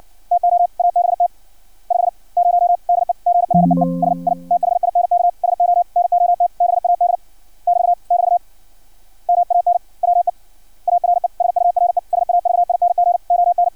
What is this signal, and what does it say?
Morse code, heard as a keyed tone, about 56 wpm, 700 Hz. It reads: TO ABT 50 DEGREES I THINK SO NOT BAD <BT> <BT> MAN WE ARE SURE SAVING ON
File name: archived / CRSnet